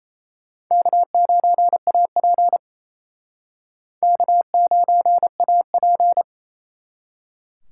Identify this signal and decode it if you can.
Morse code, heard as a keyed tone, about 37 wpm, 700 Hz. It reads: K9AP K9AP